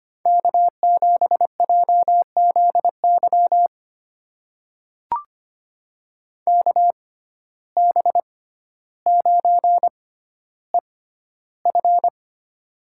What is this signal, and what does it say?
Morse code, heard as a keyed tone, about 25 wpm, spaced as for 13 wpm, 700 Hz. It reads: K7JZY KB9EF
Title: USA Call Signs - 5 Characters - 6x Character Spacing - 25wpm